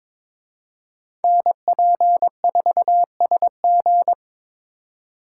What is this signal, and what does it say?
Morse code, heard as a keyed tone, about 26 wpm, 700 Hz. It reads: NP4SG
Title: Speed Racing - USA 2x2 Call Signs 22wpm